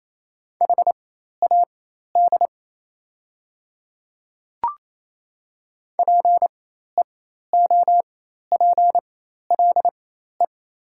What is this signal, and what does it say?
Morse code, heard as a keyed tone, about 28 wpm, spaced as for 13 wpm, 700 Hz. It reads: HAD PEOPLE
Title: 4x Character Spacing - Top 100 Words - 28wpm